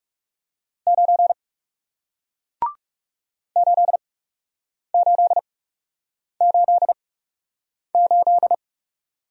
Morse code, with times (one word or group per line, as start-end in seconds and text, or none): 0.9-1.3 9
3.6-8.5 8888